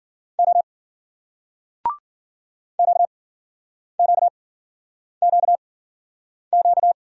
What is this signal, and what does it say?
Morse code, heard as a keyed tone, about 50 wpm, spaced as for 9 wpm, 700 Hz. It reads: O QQQQ
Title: Speed-Racing Single Letters 40wpm